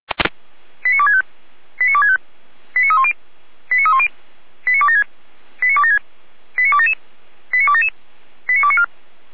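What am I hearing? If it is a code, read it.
Five-tone selective call (ZVEI), heard as quick runs of five tones, tones of about 70 ms. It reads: 89265 89265 89310 89310 89276 89276 89277 89277 89294
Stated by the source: ZVEI five-tone selective call 89173 received by SDR, from the feuerwehr-orlamuende.org fire brigade server.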